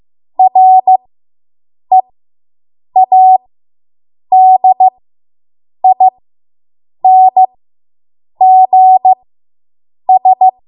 Morse code, heard as a keyed tone, about 15 wpm, spaced as for 7 wpm, 750 Hz. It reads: READINGS